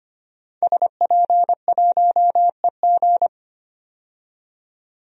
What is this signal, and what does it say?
Morse code, heard as a keyed tone, about 25 wpm, 700 Hz. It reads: SP1EG
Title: DXCC-100 Call Signs - 5 Characters - 25wpm